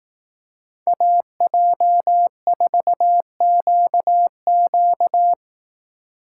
Morse code, heard as a keyed tone, about 18 wpm, 700 Hz. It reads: AJ4QQ
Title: Speed Racing - USA 2x2 Call Signs 15wpm